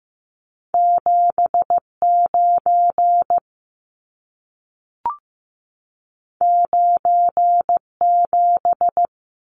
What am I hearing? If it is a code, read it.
Morse code, heard as a keyed tone, about 15 wpm, 700 Hz. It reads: 79 97
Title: Numbers - 2-Digit 15wpm